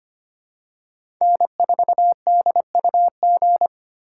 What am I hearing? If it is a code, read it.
Morse code, heard as a keyed tone, about 25 wpm, 700 Hz. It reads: N4DUG
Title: USA Call Signs - 5 Characters - 4x Character Spacing - 25wpm